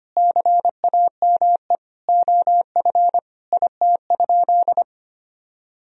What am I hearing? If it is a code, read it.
Morse code, heard as a keyed tone, about 25 wpm, 700 Hz. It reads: CAME OF IT?
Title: Sentences from Top 1000 Words - Review Part 3 - 25wpm